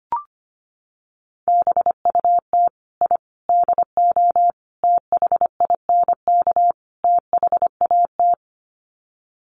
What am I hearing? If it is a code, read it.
Morse code, heard as a keyed tone, about 25 wpm, 700 Hz. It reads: BUT I DO THINK THAT
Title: Sets of 5 Words Encore2 25wpm